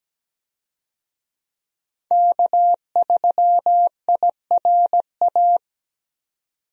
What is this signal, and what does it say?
Morse code, heard as a keyed tone, about 17 wpm, 700 Hz. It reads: K3IRA
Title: USA Call Signs - 5 Characters - 17wpm